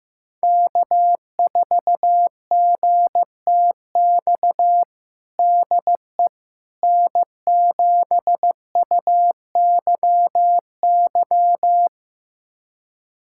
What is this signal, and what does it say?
Morse code, heard as a keyed tone, about 16 wpm, 700 Hz. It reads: K4GTX DE N7UYY
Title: USA Callsign DE Callsign 15wpm